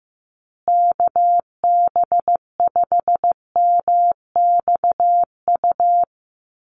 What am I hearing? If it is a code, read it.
Morse code, heard as a keyed tone, about 15 wpm, 700 Hz. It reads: KB5MXU
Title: USA Call Signs Repeated 2x 15wpm